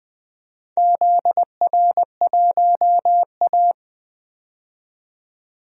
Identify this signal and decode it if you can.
Morse code, heard as a keyed tone, about 20 wpm, 700 Hz. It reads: ZR1A
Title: DXCC-100 Call Signs - 4 Characters - 20wpm